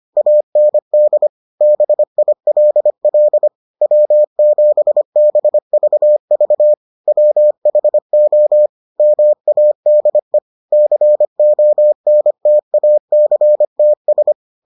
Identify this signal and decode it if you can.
Morse code, heard as a keyed tone, about 25 wpm, 600 Hz. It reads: AND BILL W7BVV WHO MADE CONTACTS